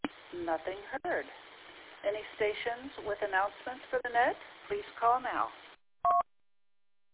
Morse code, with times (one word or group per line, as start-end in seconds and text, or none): none